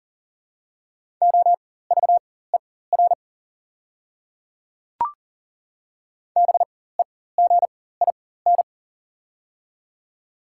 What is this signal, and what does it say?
Morse code, heard as a keyed tone, about 40 wpm, spaced as for 19 wpm, 700 Hz. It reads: OVER BEGIN
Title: 4x Character Spacing - Top 300 Words - 40wpm